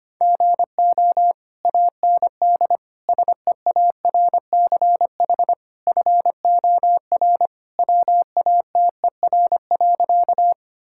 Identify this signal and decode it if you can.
Morse code, heard as a keyed tone, about 25 wpm, 700 Hz. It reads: GO AND SEARCH FOR WATER.
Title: Sentences from Top 1000 Words - Review Part 5 - 25wpm